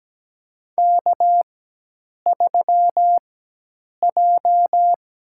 Morse code, heard as a keyed tone, about 17 wpm, spaced as for 8 wpm, 700 Hz. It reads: K3J
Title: USA Call Signs - 4x Character Spacing - 17wpm